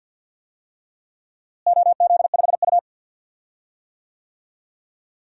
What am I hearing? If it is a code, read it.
Morse code, heard as a keyed tone, about 53 wpm, 700 Hz. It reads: OZ5U
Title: DXCC-100 Call Signs - 4 Characters - Repeated 3x - 50wpm